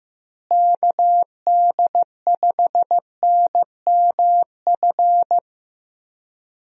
Morse code, heard as a keyed tone, about 15 wpm, 700 Hz. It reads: KD5NMF
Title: USA Call Signs - 6 Characters - Repeated 3x - 15wpm